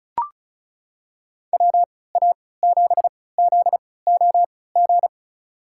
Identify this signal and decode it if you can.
Morse code, heard as a keyed tone, about 36 wpm, spaced as for 20 wpm, 700 Hz. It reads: WA7ZOG
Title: USA Call Signs - 3x Character Spacing - 35wpm